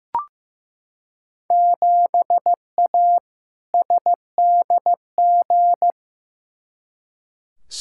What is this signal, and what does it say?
Morse code, heard as a keyed tone, about 15 wpm, 700 Hz. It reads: 7A SDG
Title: Field Day Exchanges Encore - 15wpm